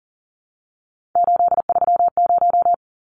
Morse code, heard as a keyed tone, about 40 wpm, 700 Hz. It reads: 830
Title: Numbers - 3-Digit 40wpm